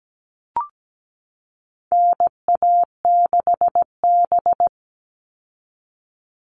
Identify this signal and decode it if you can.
Morse code, heard as a keyed tone, about 17 wpm, 700 Hz. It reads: NA6B
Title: All Cut Numbers - 4-Digit 17wpm